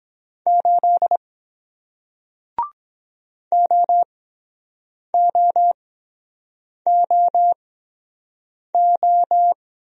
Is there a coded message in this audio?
Yes: Morse code from 0.5 to 9.5 s, heard as a keyed tone, about 22 wpm, spaced as for 7 wpm, 700 Hz. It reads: 8 OOOO